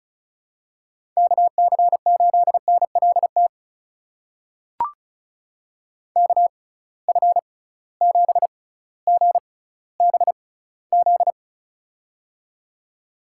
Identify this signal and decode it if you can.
Morse code, heard as a keyed tone, about 36 wpm, spaced as for 18 wpm, 700 Hz. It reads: KC8NLT KF7GBZ
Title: USA Call Signs - 6x Character Spacing - 35wpm